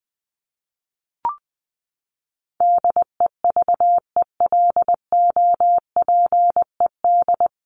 Morse code, heard as a keyed tone, about 20 wpm, 700 Hz. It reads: DEVELOPED
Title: ED Suffix - Top 100 Words with Base Words 20wpm